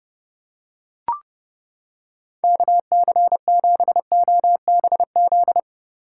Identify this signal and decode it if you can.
Morse code, heard as a keyed tone, about 28 wpm, 700 Hz. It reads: KC7OBZ